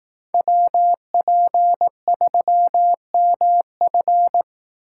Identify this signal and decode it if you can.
Morse code, heard as a keyed tone, about 18 wpm, 700 Hz. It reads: WP3MF
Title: Speed Racing - USA 2x2 Call Signs 15wpm